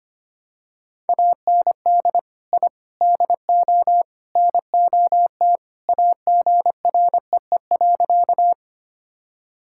Morse code, heard as a keyed tone, about 25 wpm, 700 Hz. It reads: AND I DO NOT AGREE.
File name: practice / 25wpm / Sentences with 4-Word Phrases - 25wpm